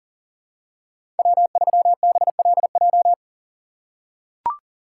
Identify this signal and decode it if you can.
Morse code, heard as a keyed tone, about 40 wpm, 700 Hz. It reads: W3BLJ
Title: USA Call Signs - 5 Characters - 3x Character Spacing - 40wpm